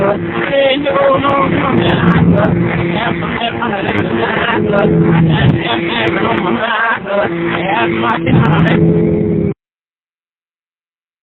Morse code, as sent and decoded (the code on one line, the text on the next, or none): none